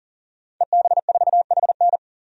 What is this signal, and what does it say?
Morse code, heard as a keyed tone, about 40 wpm, 700 Hz. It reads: EB4HN
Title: DXCC-100 Call Signs - 5 Characters - 40wpm